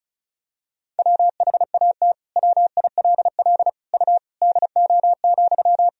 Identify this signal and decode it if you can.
Morse code, heard as a keyed tone, about 34 wpm, 700 Hz. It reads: WHAT WILL U DO,
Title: Sentences from Top 100 Words and Abbreviations 35wpm